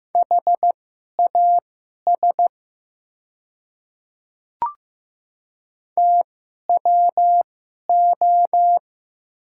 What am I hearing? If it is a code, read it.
Morse code, heard as a keyed tone, about 15 wpm, spaced as for 11 wpm, 700 Hz. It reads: HAS TWO
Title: 2x Character Spacing - Top 100 Words - 15wpm